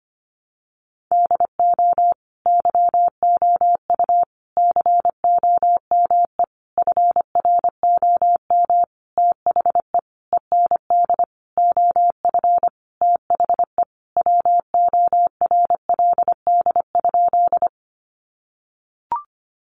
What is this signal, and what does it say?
Morse code, heard as a keyed tone, about 25 wpm, 700 Hz. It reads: DO YOU COME FROM THE END OF THE WORLD?
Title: Sentences from Top 200 Words 25wpm